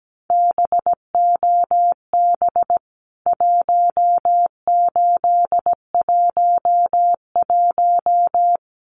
Morse code, heard as a keyed tone, about 17 wpm, 700 Hz. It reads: BOB 1811
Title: CWT Exchange 17wpm